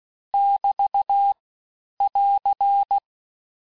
Morse code, heard as a keyed tone, about 16 wpm, 800 Hz. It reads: <BT> <AR>
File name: cours-du-lundi files / Cours-1586-texte-chiffre-vitesse-15mpm